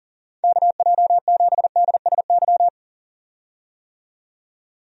Morse code, heard as a keyed tone, about 40 wpm, 700 Hz. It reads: KJ7DSY
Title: USA Call Signs - 6 Characters - 40wpm